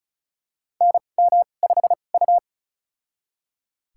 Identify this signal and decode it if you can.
Morse code, heard as a keyed tone, about 36 wpm, spaced as for 25 wpm, 700 Hz. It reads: NM5U
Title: USA Call Signs - 4 Characters - 2x Character Spacing - 35wpm